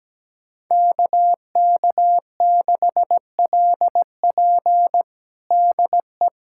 Morse code, heard as a keyed tone, about 17 wpm, 700 Hz. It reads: KK6LP DE
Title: USA Callsign DE Callsign - 5 Characters - 17wpm